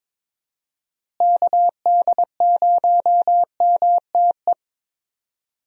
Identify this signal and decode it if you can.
Morse code, heard as a keyed tone, about 22 wpm, 700 Hz. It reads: KD0MTE
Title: USA Call Signs - 6 Characters - Repeated 2x - 22wpm